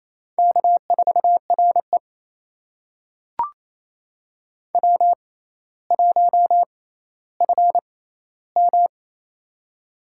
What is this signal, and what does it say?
Morse code, heard as a keyed tone, about 28 wpm, spaced as for 14 wpm, 700 Hz. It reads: K4RE W1FM
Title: USA Call Signs - 4 Characters - 6x Character Spacing - 28wpm